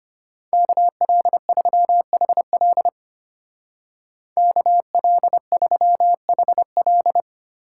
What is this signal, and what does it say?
Morse code, heard as a keyed tone, about 31 wpm, 700 Hz. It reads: KL3HL KL3HL
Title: Speed Racing - USA 2x2 Call Signs 25wpm